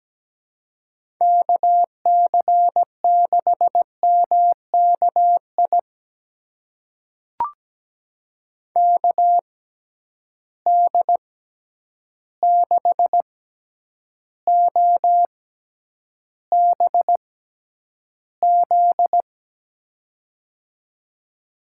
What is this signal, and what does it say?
Morse code, heard as a keyed tone, about 17 wpm, spaced as for 9 wpm, 700 Hz. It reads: KC6MKI KD6OBZ
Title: USA Call Signs - 6x Character Spacing - 17wpm